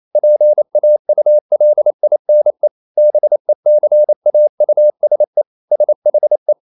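Morse code, heard as a keyed tone, about 28 wpm, 600 Hz. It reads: PAULINE BECAUSE SHE
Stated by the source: Morse code practice at 28 wpm, text of Ken Follett's novel Never in English, from CW med Gnister.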